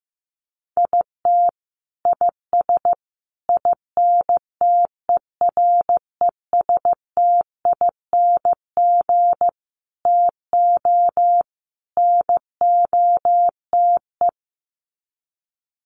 Morse code, heard as a keyed tone, about 15 wpm, 700 Hz. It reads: IT IS INTERESTING TO NOTE
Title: Sets of 5 Words Encore2 15wpm